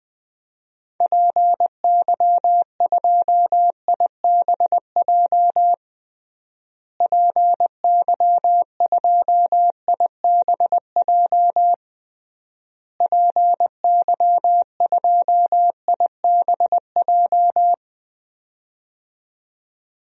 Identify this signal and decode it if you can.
Morse code, heard as a keyed tone, about 20 wpm, 700 Hz. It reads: PY2IBJ PY2IBJ PY2IBJ
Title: DXCC-100 Call Signs - 6 Characters - Repeated 3x - 20wpm